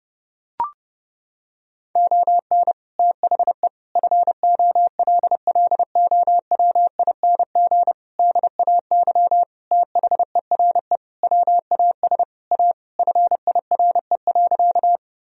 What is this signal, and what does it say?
Morse code, heard as a keyed tone, about 30 wpm, 700 Hz. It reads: ON THE FOLLOWING DAY THERE WAS A FIRE.
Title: Sentences from Top 1000 Words - Review Part 4 - 30wpm